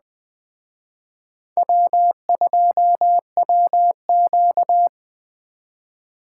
Morse code, heard as a keyed tone, about 20 wpm, 700 Hz. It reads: W2WQ
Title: Speed Racing - USA 1x2 Call Signs 17wpm